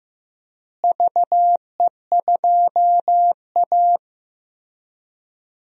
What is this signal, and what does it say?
Morse code, heard as a keyed tone, about 15 wpm, 700 Hz. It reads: VE2A
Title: DXCC-100 Call Signs - 4 Characters - Repeated 3x - 15wpm